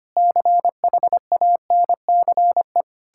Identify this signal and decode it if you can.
Morse code, heard as a keyed tone, about 25 wpm, 700 Hz. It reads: CHANCE
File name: practice / 25wpm / Difficult Sentences with Hints - 25wpm